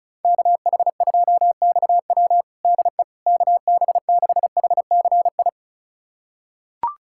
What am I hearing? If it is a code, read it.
Morse code, heard as a keyed tone, about 34 wpm, 700 Hz. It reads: KH2XW DE KB6HCI
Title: USA Callsign DE Callsign 35wpm